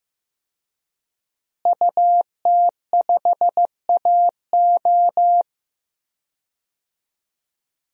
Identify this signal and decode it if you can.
Morse code, heard as a keyed tone, about 15 wpm, 700 Hz. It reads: UT5AO
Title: DXCC-100 Call Signs - 5 Characters - Repeated 3x - 15wpm